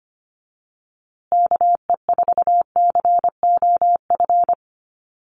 Morse code, heard as a keyed tone, about 25 wpm, 700 Hz. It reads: KE4COF
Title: USA Call Signs Repeated 3x 25wpm